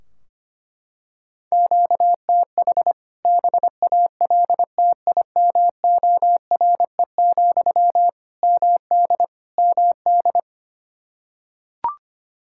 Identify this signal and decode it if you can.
Morse code, heard as a keyed tone, about 25 wpm, 700 Hz. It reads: QTH BALTIMORE, MD MD
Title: K1USN SST QSO Practice 25wpm